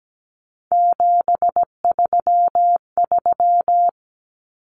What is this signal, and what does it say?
Morse code, heard as a keyed tone, about 17 wpm, 700 Hz. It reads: 733
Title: Common Cut Numbers - 3-Digit 17wpm